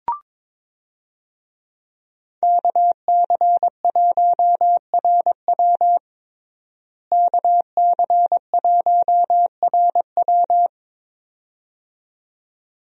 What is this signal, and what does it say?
Morse code, heard as a keyed tone, about 21 wpm, 700 Hz. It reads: KC1RW KC1RW
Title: USA Call Signs - 5 Characters - Repeated 2x - 22wpm